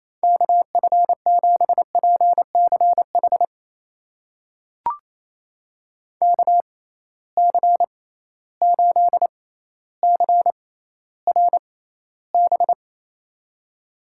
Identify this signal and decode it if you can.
Morse code, heard as a keyed tone, about 28 wpm, spaced as for 14 wpm, 700 Hz. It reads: KF7PCH KC8CRB
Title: USA Call Signs - 6 Characters - 6x Character Spacing - 28wpm